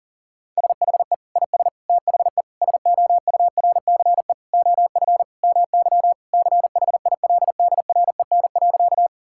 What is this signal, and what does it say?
Morse code, heard as a keyed tone, about 40 wpm, 700 Hz. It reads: SHE IS THE SOURCE OF MY CHILDREN.
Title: Sentences from Top 1000 Words - Review Part 5 - 40wpm